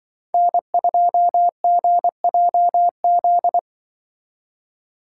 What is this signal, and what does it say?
Morse code, heard as a keyed tone, about 24 wpm, 700 Hz. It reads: N2GJZ
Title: Speed Racing - USA 1x3 Call Signs 20wpm